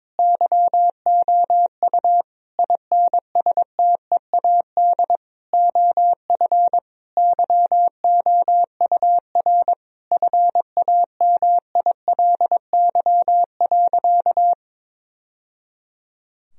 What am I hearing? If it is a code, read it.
Morse code, heard as a keyed tone, about 22 wpm, 700 Hz. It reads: YOU INSTEAD OF YOUR FAMILY.
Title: Sentences from Top 1000 Words - Review Part 4 - 22wpm